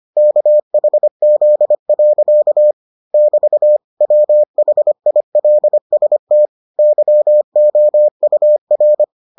Morse code, heard as a keyed tone, about 25 wpm, 600 Hz. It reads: KHZ. <BT> WHILST YOUR